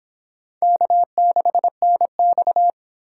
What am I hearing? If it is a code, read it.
Morse code, heard as a keyed tone, about 31 wpm, 700 Hz. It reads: K6NX